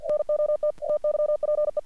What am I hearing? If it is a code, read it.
Morse code, heard as a keyed tone, about 48 wpm, 600 Hz. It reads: NOT TYPE
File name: archived / CRSnet